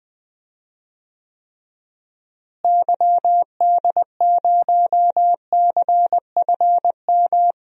Morse code, heard as a keyed tone, about 20 wpm, 700 Hz. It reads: YD0CFM
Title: DXCC-100 Call Signs - 6 Characters - Repeated 3x - 20wpm